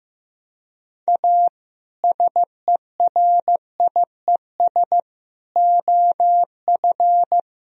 Morse code, heard as a keyed tone, about 15 wpm, 700 Hz. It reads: A SERIES OF
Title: Speed-Racing - Sets of 3 Words 15wpm